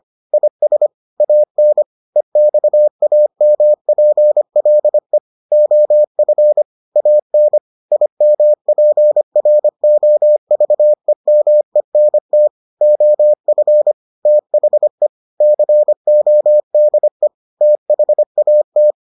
Morse code, heard as a keyed tone, about 25 wpm, 600 Hz. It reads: IS AN EXAMPLE OF AN IMPROVEMENT OF THE CODE THAT